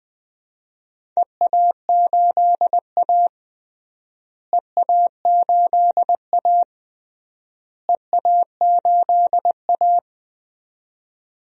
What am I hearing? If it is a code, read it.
Morse code, heard as a keyed tone, about 20 wpm, 700 Hz. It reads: EA8A EA8A EA8A
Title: DXCC-100 Call Signs - 4 Characters - Repeated 3x - 20wpm